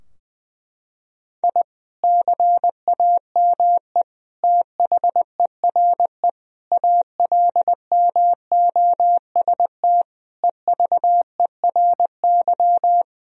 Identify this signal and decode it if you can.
Morse code, heard as a keyed tone, about 20 wpm, 700 Hz. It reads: I CAME THERE ALMOST EVERY
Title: Sentences from Top 1000 Words - Review Part 4 - 20wpm